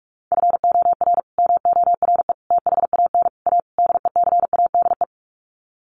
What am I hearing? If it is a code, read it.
Morse code, heard as a keyed tone, about 44 wpm, 700 Hz. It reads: FOR MORE THAN A DECADE
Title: Sets of 5 Words 45wpm